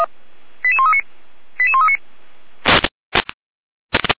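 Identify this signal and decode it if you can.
Five-tone selective call (ZVEI), heard as quick runs of five tones, tones of about 70 ms. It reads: 88139 88139
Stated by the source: ZVEI five-tone call, address 88407, received on an SDR.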